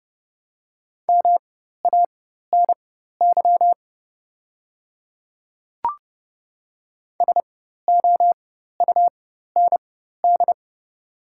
Morse code, heard as a keyed tone, about 30 wpm, spaced as for 14 wpm, 690 Hz. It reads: MANY SOUND
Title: 4x Character Spacing - Top 100 Words - 30wpm